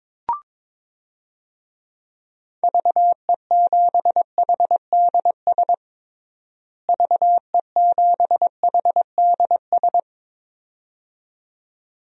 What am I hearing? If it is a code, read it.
Morse code, heard as a keyed tone, about 22 wpm, 700 Hz. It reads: VE7HDS VE7HDS
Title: DXCC-100 Call Signs - 6 Characters - Repeated 2x - 22wpm